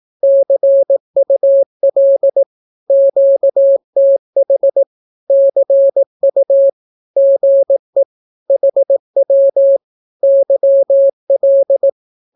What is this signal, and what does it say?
Morse code, heard as a keyed tone, about 18 wpm, 550 Hz. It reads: CUL QTH CU GE HW YL
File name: naukapodstaw / skroty / skroty